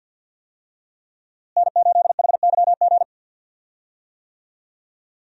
Morse code, heard as a keyed tone, about 51 wpm, 700 Hz. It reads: N8HYG
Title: USA Call Signs - 5 Characters - Repeated 2x - 50wpm